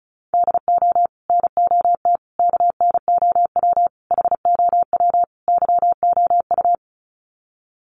Morse code, heard as a keyed tone, about 34 wpm, 700 Hz. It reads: DO NOT KNOW HOW YOU
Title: Sets of 5 Words Encore 35wpm